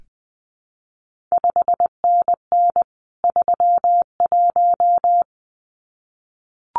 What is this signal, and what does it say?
Morse code, heard as a keyed tone, about 19 wpm, 700 Hz. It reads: 5NN 31
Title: CQ WW DX Contest Exchange - 20wpm